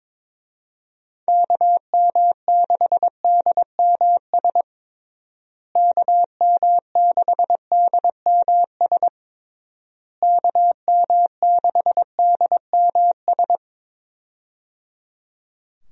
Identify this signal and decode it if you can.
Morse code, heard as a keyed tone, about 21 wpm, 700 Hz. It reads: KM6DMS KM6DMS KM6DMS
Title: USA Call Signs - 6 Characters - Repeated 3x - 22wpm